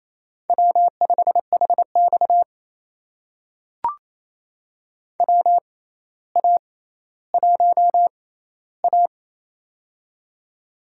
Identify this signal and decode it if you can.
Morse code, heard as a keyed tone, about 28 wpm, spaced as for 14 wpm, 700 Hz. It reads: W5HX WA1A
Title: USA Call Signs - 4 Characters - 6x Character Spacing - 28wpm